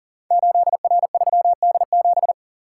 Morse code, heard as a keyed tone, about 42 wpm, 700 Hz. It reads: 8R3D7
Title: Code Groups - 5 Letter:Numbers - 40wpm